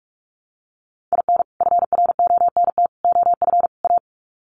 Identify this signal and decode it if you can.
Morse code, heard as a keyed tone, about 44 wpm, 700 Hz. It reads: IN FRONT OF A
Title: Sets of 4 Words 45wpm